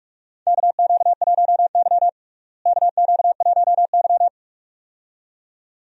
Morse code, heard as a keyed tone, about 44 wpm, 700 Hz. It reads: KQ1Y KQ1Y
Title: USA Call Signs - 4 Characters - Repeated 2x - 45wpm